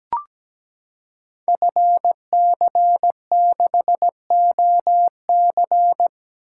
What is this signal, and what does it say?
Morse code, heard as a keyed tone, about 17 wpm, 700 Hz. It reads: FC6OC
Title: Code Groups - 5 Letter:Numbers - 17wpm